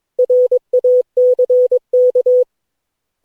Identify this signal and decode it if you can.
Morse code, heard as a keyed tone, about 22 wpm, 500 Hz. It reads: RACK
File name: cw / Session9 cw training